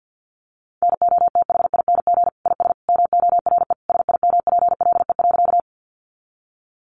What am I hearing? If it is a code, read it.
Morse code, heard as a keyed tone, about 50 wpm, 700 Hz. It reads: NOTHING IS MORE SIMPLE.
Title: Sentences from Top 500 Words 50wpm